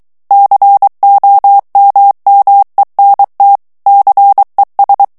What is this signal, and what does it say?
Morse code, heard as a keyed tone, about 23 wpm, 800 Hz. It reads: COMMENT CES
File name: FRANCAIS TEXTE CLAIR 2 VITESSE 22 MOTS MINUTES